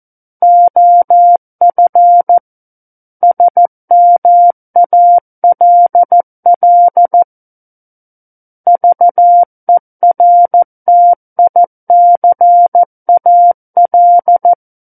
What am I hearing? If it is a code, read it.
Morse code, heard as a keyed tone, about 14 wpm, 700 Hz. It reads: OF SMALL VERTICAL
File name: Book One / 05-Chapter 5 - 14 WPM